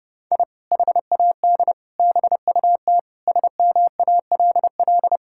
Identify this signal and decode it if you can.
Morse code, heard as a keyed tone, about 30 wpm, 700 Hz. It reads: I HAD BUT SMALL
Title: Sentences from Top 1000 Words - Review Part 6 - 30wpm